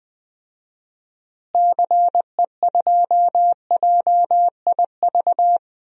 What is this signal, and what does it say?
Morse code, heard as a keyed tone, about 20 wpm, 700 Hz. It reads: CE2JIV
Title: DXCC-100 Call Signs - 6 Characters - Repeated 3x - 20wpm